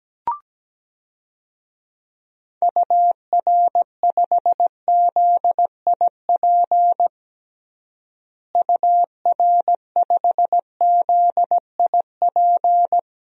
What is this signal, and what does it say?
Morse code, heard as a keyed tone, about 17 wpm, 700 Hz. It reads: UR5ZIP UR5ZIP
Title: DXCC-100 Call Signs - 6 Characters - Repeated 3x - 17wpm